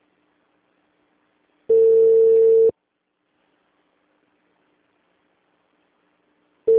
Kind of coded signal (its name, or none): telephone call progress tone